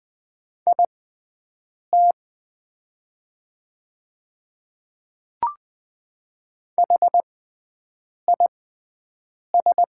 Morse code, heard as a keyed tone, about 20 wpm, spaced as for 7 wpm, 700 Hz. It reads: IT HIS
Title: 6x Character Spacing - Top 100 Words - 20wpm